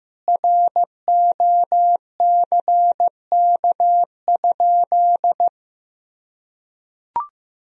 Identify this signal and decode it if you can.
Morse code, heard as a keyed tone, about 16 wpm, 700 Hz. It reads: ROCK?
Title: Sentences from Top 1000 Words - Review Part 2 - 15wpm